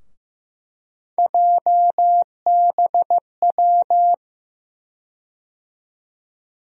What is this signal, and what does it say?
Morse code, heard as a keyed tone, about 15 wpm, 700 Hz. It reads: JBW
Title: USA Call Sign Postfixes - 3 Letters 15wpm